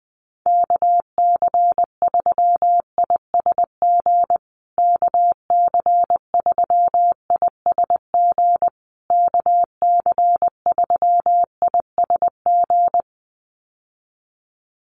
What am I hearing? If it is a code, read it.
Morse code, heard as a keyed tone, about 20 wpm, 700 Hz. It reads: KC3ISG KC3ISG KC3ISG
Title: USA Call Signs Repeated 3x 20wpm